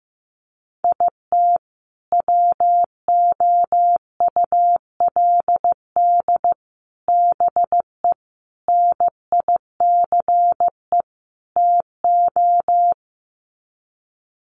Morse code, heard as a keyed tone, about 15 wpm, 700 Hz. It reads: IT WOULD BE NICE TO
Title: Sets of 5 Words Encore2 15wpm